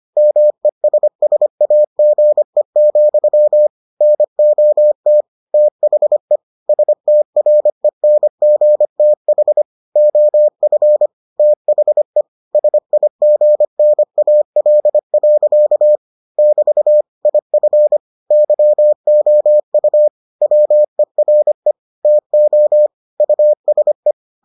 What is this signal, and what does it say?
Morse code, heard as a keyed tone, about 25 wpm, 600 Hz. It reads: MESSAGE, NOT THE STRENGTH OF THE SIGNAL. <BT> IF YOU WERE TO USE